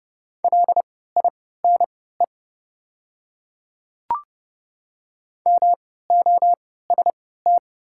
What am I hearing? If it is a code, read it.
Morse code, heard as a keyed tone, about 30 wpm, spaced as for 17 wpm, 700 Hz. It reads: LINE MOST
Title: 3x Character Spacing - Top 200 Words - 30wpm